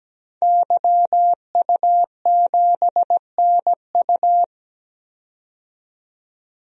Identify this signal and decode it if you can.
Morse code, heard as a keyed tone, about 17 wpm, 700 Hz. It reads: YU7NU
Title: DXCC-100 Call Signs - 5 Characters - 17wpm